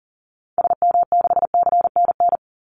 Morse code, heard as a keyed tone, about 40 wpm, 700 Hz. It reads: SM6CNN
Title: Call Signs - 40wpm